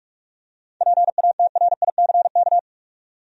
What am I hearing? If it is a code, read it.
Morse code, heard as a keyed tone, about 43 wpm, 700 Hz. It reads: PATRICK
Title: USA Common Names - 45wpm